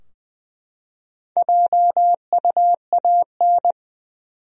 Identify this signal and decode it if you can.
Morse code, heard as a keyed tone, about 20 wpm, 700 Hz. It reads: JUAN
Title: USA Common Names - Max 5 Letters - 20wpm